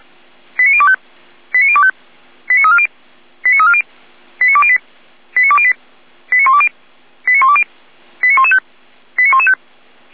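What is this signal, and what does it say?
Five-tone selective call (ZVEI), heard as quick runs of five tones, tones of about 70 ms. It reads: 89025 89025 89340 89340 89208 89208 89120 89120 89115 89115
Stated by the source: ZVEI five-tone call 89270 received by SDR.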